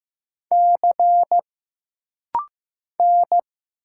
Morse code, heard as a keyed tone, about 15 wpm, 700 Hz. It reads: C N
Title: Single Letters - Rapid-Fire 15wpm